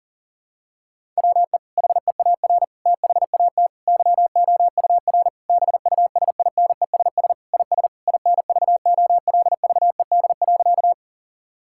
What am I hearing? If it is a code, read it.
Morse code, heard as a keyed tone, about 40 wpm, 700 Hz. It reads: WE HEAR THAT YOUR BUSINESS IS INVOLVED.
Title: Sentences from Top 1000 Words - Review Part 5 - 40wpm